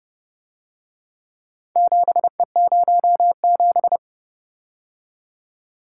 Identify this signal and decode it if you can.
Morse code, heard as a keyed tone, about 30 wpm, 700 Hz. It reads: 7E07